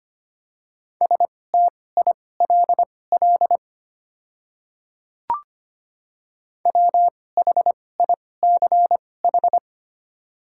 Morse code, heard as a keyed tone, about 25 wpm, spaced as for 18 wpm, 700 Hz. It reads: STILL WHICH
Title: 2x Character Spacing - Top 500 Words - 25wpm